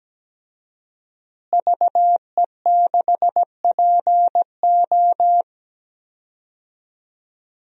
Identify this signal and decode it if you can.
Morse code, heard as a keyed tone, about 17 wpm, 700 Hz. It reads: VE6PO
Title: DXCC-100 Call Signs - 5 Characters - Repeated 3x - 17wpm